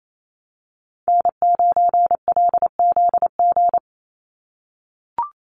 Morse code, heard as a keyed tone, about 28 wpm, 700 Hz. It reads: N9LZG
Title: USA Call Signs 28wpm